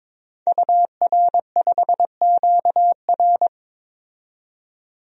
Morse code, heard as a keyed tone, about 22 wpm, 700 Hz. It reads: UR5QR
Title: DXCC-100 Call Signs - 5 Characters - 22wpm